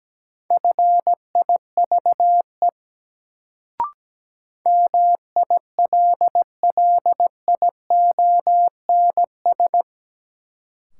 Morse code, heard as a keyed tone, about 17 wpm, 700 Hz. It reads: FIVE MILLIONS
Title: Sets of 5 Words - New Word Drill - 17wpm